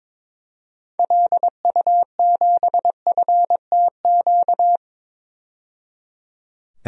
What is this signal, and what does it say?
Morse code, heard as a keyed tone, about 22 wpm, 700 Hz. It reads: LU7FTQ
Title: DXCC-100 Call Signs - 6 Characters - 22wpm